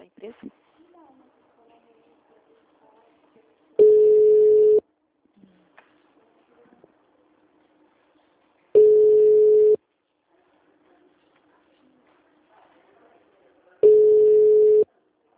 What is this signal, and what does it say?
Ringback tone (Europe, 425 Hz), with speech.
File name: out- -2008- - - .2256